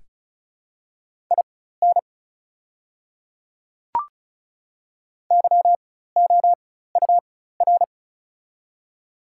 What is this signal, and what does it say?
Morse code, heard as a keyed tone, about 36 wpm, spaced as for 16 wpm, 690 Hz. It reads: IN YOUR
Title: 4x Character Spacing - Top 100 Words - 35wpm